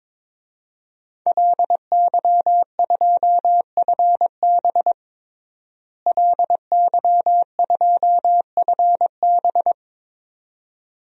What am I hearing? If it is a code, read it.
Morse code, heard as a keyed tone, about 22 wpm, 700 Hz. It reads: LY2FB LY2FB
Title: DXCC-100 Call Signs - 5 Characters - Repeated 2x - 22wpm